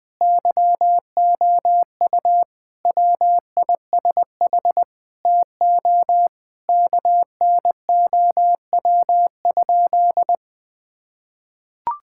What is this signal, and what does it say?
Morse code, heard as a keyed tone, about 20 wpm, 700 Hz. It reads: YOU WISH TO KNOW?
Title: Sentences from Top 1600 Words 20wpm